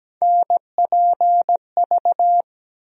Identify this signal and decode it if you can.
Morse code, heard as a keyed tone, about 17 wpm, 700 Hz. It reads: NPV
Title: USA Call Sign Postfixes - 3 Letters 17wpm